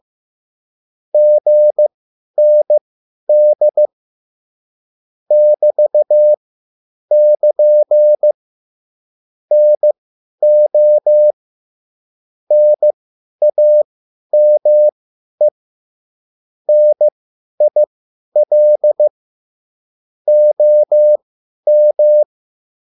Morse code, heard as a keyed tone, about 15 wpm, spaced as for 10 wpm, 600 Hz. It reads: GND <BT><KN> NO NAME NIL OM
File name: cwbook